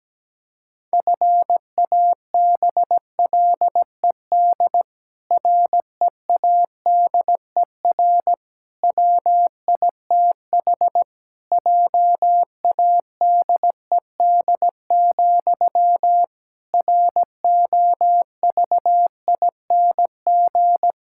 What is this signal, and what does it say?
Morse code, heard as a keyed tone, about 17 wpm, 700 Hz. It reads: FABLED READER WITH JADED, ROVING
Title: Pangram Challenge v2 - 17wpm